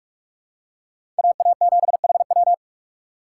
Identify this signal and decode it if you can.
Morse code, heard as a keyed tone, about 46 wpm, 700 Hz. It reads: AA7HW
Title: USA Call Signs - 5 Characters - 6x Character Spacing - 45wpm